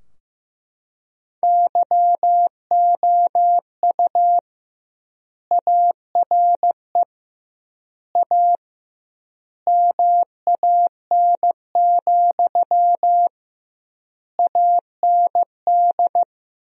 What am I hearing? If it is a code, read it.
Morse code, heard as a keyed tone, about 15 wpm, 700 Hz. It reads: YOU ARE A MAN, AND
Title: 2x Word-Spacing - Sentences from Top 200 Words - 15wpm